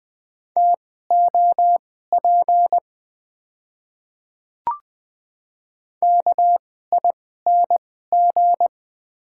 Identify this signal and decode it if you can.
Morse code, heard as a keyed tone, about 20 wpm, spaced as for 14 wpm, 690 Hz. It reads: TOP KING
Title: 2x Character Spacing - Top 500 Words - 20wpm